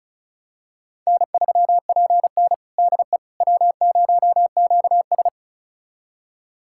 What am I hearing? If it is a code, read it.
Morse code, heard as a keyed tone, about 34 wpm, 700 Hz. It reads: N3PN DE W0QS
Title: USA Callsign DE Callsign - 4 Characters - 35wpm